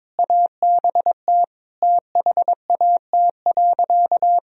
Morse code, heard as a keyed tone, about 22 wpm, 700 Hz. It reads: ABT THAT.
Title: Sentences from Top 400 Words and Abbreviations 22wpm